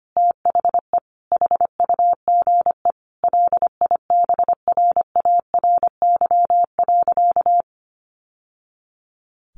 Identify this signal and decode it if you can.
Morse code, heard as a keyed tone, about 25 wpm, 700 Hz. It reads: THE HUGE LIBRARY.
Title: Sentences from Top 1100 Words 25wpm